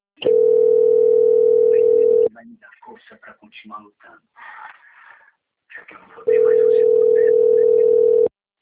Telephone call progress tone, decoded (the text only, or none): ringback tone